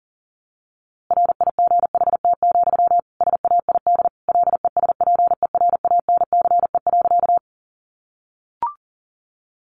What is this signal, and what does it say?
Morse code, heard as a keyed tone, about 40 wpm, 700 Hz. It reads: RIGHT, SAID LESPERANCE.
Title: A Sound Of Thunder 40wpm